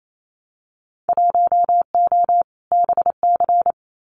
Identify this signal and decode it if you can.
Morse code, heard as a keyed tone, about 28 wpm, 700 Hz. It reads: 1O BC
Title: Winter Field Day Exchanges Encore 28wpm